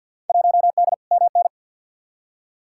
Morse code, heard as a keyed tone, about 53 wpm, 700 Hz. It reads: 1D MN
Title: Field Day Exchanges - 50wpm